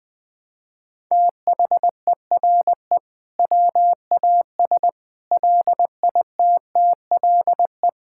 Morse code, heard as a keyed tone, about 21 wpm, 700 Hz. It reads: THERE WAS LITTLE
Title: Sentences from Top 1000 Words - Review Part 5 - 20wpm